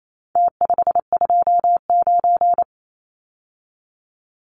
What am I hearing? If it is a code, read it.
Morse code, heard as a keyed tone, about 28 wpm, 700 Hz. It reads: T529
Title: Numbers - Contest Style - 4-Digit 28wpm